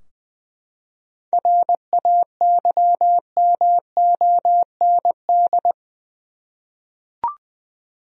Morse code, heard as a keyed tone, about 20 wpm, 700 Hz. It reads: RAYMOND
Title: USA Common Names - 20wpm